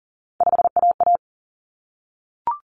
Morse code, heard as a keyed tone, about 40 wpm, 700 Hz. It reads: 5AA
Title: Common Cut Numbers - 3-Digit 40wpm